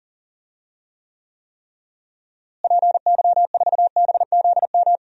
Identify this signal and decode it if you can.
Morse code, heard as a keyed tone, about 40 wpm, 700 Hz. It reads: PY4BZM
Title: DXCC-100 Call Signs - 6 Characters - Repeated 3x - 40wpm